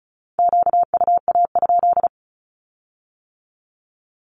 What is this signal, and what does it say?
Morse code, heard as a keyed tone, about 36 wpm, 700 Hz. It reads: QUA?
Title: Q-Codes with Spoken Meaning - 35wpm